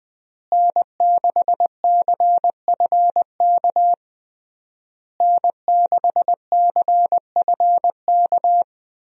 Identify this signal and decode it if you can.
Morse code, heard as a keyed tone, about 20 wpm, 700 Hz. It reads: N6CFK N6CFK
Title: USA Call Signs - 5 Characters - Repeated 3x - 20wpm